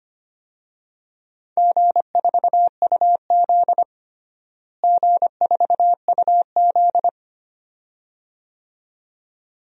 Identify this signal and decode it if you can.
Morse code, heard as a keyed tone, about 25 wpm, 700 Hz. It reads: G4UZ G4UZ
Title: DXCC-100 Call Signs - 4 Characters - Repeated 2x - 25wpm